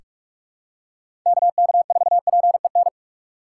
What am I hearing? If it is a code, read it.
Morse code, heard as a keyed tone, about 41 wpm, 700 Hz. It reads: KK4PEN
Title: USA Call Signs - 2x Character Spacing - 45wpm